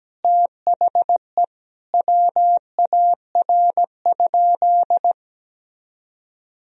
Morse code, heard as a keyed tone, about 17 wpm, 700 Hz. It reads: THE WAR?
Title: Sentences from Top 1000 Words - Review Part 2 - 17wpm